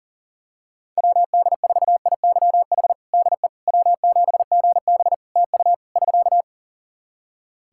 Morse code, heard as a keyed tone, about 40 wpm, 700 Hz. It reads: WD4IYH DE W7GB TU <SK>
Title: Practice QSO - Exchange 4 - Ending - 40wpm